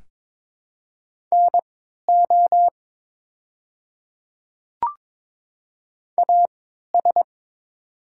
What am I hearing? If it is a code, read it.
Morse code, heard as a keyed tone, about 22 wpm, spaced as for 12 wpm, 700 Hz. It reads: NO AS
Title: 3x Character Spacing - Top 100 Words - 22wpm